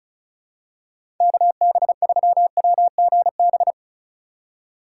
Morse code, heard as a keyed tone, about 36 wpm, 700 Hz. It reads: KB3WGB
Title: USA Call Signs - 6 Characters - 4x Character Spacing - 35wpm